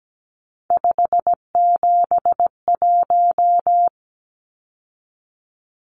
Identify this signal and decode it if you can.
Morse code, heard as a keyed tone, about 17 wpm, 700 Hz. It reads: 571
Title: Numbers - 3-Digit 17wpm